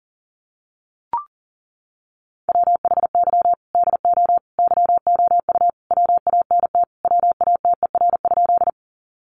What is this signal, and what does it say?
Morse code, heard as a keyed tone, about 40 wpm, 700 Hz. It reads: WHY DO YOU WANT WATER?
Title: Sentences from Top 200 Words 40wpm